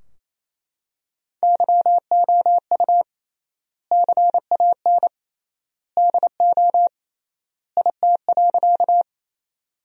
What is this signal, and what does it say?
Morse code, heard as a keyed tone, about 28 wpm, 700 Hz. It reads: YOU CAN DO IT.
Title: 3x Word-Spacing - Sentences from Top 100 Words - 28wpm